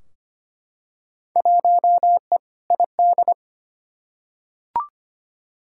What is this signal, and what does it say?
Morse code, heard as a keyed tone, about 25 wpm, 700 Hz. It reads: 1E ID
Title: Field Day Exchanges - 25wpm